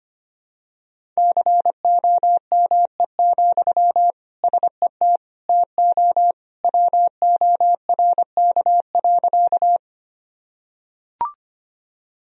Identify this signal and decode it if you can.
Morse code, heard as a keyed tone, about 25 wpm, 700 Hz. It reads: COME, SET TO WORK.
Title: Sentences from Top 1000 Words - Review Part 1 - 25wpm